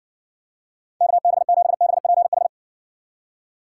Morse code, heard as a keyed tone, about 68 wpm, 700 Hz. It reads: KB7BPH